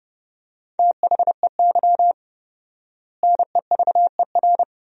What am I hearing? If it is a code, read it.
Morse code, heard as a keyed tone, about 30 wpm, 700 Hz. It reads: THEY NEVER
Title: 4x Word-Spacing - Sentences from Top 300 Words - 30wpm